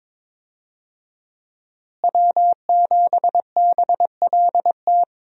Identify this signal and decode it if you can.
Morse code, heard as a keyed tone, about 22 wpm, 700 Hz. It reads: W7BLT
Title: USA Call Signs - 5 Characters - Repeated 2x - 22wpm